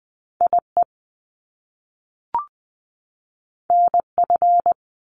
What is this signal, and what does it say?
Morse code, heard as a keyed tone, about 19 wpm, 700 Hz. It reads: I E N F
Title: Top Two Letter Combinations in Words 20wpm